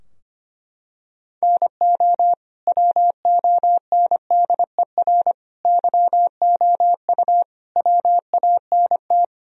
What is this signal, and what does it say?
Morse code, heard as a keyed tone, about 25 wpm, 700 Hz. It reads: NO WONDER YOU WANT
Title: Sentences from Top 1000 Words - Review Part 3 - 25wpm